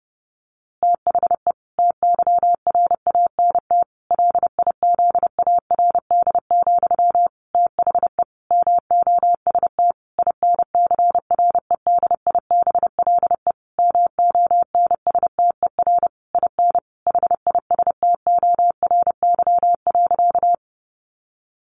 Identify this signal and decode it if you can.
Morse code, heard as a keyed tone, about 30 wpm, 700 Hz. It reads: THE TYRANT LIZARD, THE MOST INCREDIBLE MONSTER IN HISTORY.
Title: A Sound Of Thunder 30wpm